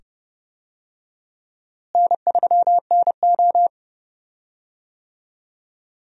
Morse code, heard as a keyed tone, about 30 wpm, 700 Hz. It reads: N3NO
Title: USA Call Signs - 4 Characters - Repeated 2x - 30wpm